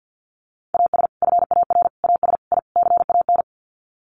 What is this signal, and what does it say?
Morse code, heard as a keyed tone, about 50 wpm, 700 Hz. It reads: AS FAR AS I CAN
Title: Sets of 5 Words Encore 50wpm